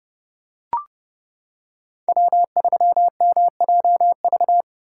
Morse code, heard as a keyed tone, about 29 wpm, 700 Hz. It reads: W3MJV